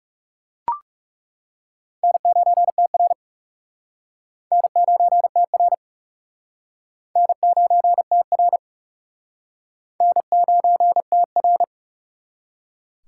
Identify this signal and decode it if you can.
Morse code, heard as a keyed tone, about 40 wpm, 700 Hz. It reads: N9TR N9TR N9TR N9TR